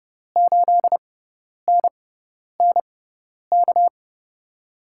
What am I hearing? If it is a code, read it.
Morse code, heard as a keyed tone, about 32 wpm, spaced as for 10 wpm, 700 Hz. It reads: 8NNK
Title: USA Call Signs - 5 Characters - 6x Character Spacing - 30wpm